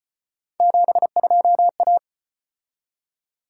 Morse code, heard as a keyed tone, about 34 wpm, 700 Hz. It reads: 72A